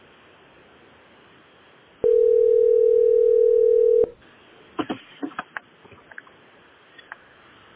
Ringback tone (North America), 440 + 480 Hz.